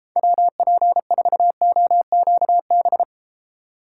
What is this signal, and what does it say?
Morse code, heard as a keyed tone, about 34 wpm, 700 Hz. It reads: WP4OQB